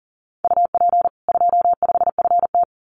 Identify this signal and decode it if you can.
Morse code, heard as a keyed tone, about 40 wpm, 700 Hz. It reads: UP 25FT
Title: Common QSO Phrases 40wpm